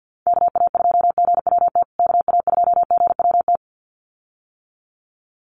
Morse code, heard as a keyed tone, about 50 wpm, 700 Hz. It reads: KA2GWT KA2GWT
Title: USA Call Signs Repeated 2x 50wpm